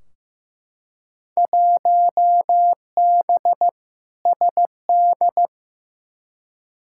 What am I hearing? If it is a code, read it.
Morse code, heard as a keyed tone, about 15 wpm, 700 Hz. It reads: 1B SD
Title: Field Day Exchanges Encore - 15wpm